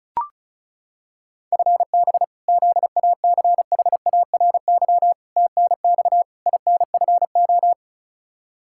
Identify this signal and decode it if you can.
Morse code, heard as a keyed tone, about 36 wpm, 700 Hz. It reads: FB ZACHARY TNX INFO
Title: Practice QSO - Exchange 3 - More Info - 35wpm